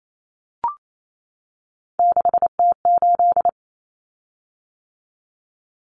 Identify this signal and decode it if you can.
Morse code, heard as a keyed tone, about 28 wpm, 700 Hz. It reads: 6T8
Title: Common Cut Numbers - 3-Digit 28wpm